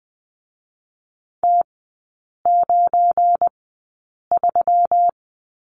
Morse code, heard as a keyed tone, about 20 wpm, 700 Hz. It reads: T 9 3
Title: 218 Lesson 35e - Number 0 - Three Characters 20wpm